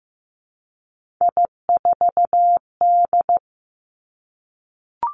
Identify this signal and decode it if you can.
Morse code, heard as a keyed tone, about 15 wpm, 700 Hz. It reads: I4D
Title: Three Challenging Letter-Number Combinations 15wpm